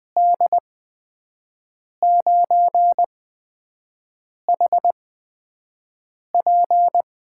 Morse code, heard as a keyed tone, about 20 wpm, spaced as for 5 wpm, 700 Hz. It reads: D9HP
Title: USA Call Signs - 5 Characters - 8x Character Spacing - 20wpm